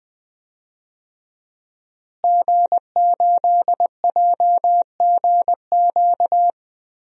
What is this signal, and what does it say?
Morse code, heard as a keyed tone, about 20 wpm, 700 Hz. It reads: G8JGQ
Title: DXCC-100 Call Signs - 5 Characters - Repeated 2x - 20wpm